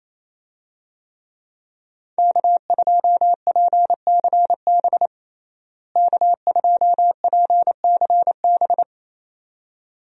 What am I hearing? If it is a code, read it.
Morse code, heard as a keyed tone, about 28 wpm, 700 Hz. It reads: K2PCB K2PCB
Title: USA Call Signs - 5 Characters - Repeated 2x - 28wpm